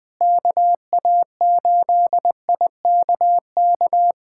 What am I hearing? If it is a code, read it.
Morse code, heard as a keyed tone, about 21 wpm, 700 Hz. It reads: KA8IKK
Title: USA Callsign DE Callsign - 6 Characters - 20wpm